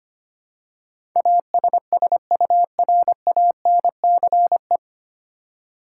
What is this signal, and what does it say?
Morse code, heard as a keyed tone, about 25 wpm, 700 Hz. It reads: ASSURANCE
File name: Nine Letter Words - 25wpm